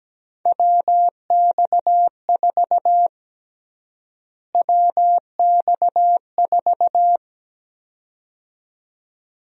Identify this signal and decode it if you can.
Morse code, heard as a keyed tone, about 17 wpm, 700 Hz. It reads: WX4 WX4
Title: USA Call Sign Prefixes - Repeated 2x - 17wpm